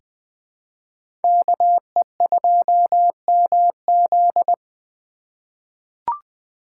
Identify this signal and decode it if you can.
Morse code, heard as a keyed tone, about 21 wpm, 700 Hz. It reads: KE2MZ
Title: USA Call Signs - 5 Characters - 3x Character Spacing - 20wpm